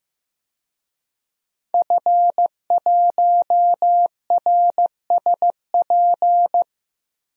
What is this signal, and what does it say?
Morse code, heard as a keyed tone, about 15 wpm, 700 Hz. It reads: F1RSP
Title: DXCC-100 Call Signs - 5 Characters - 15wpm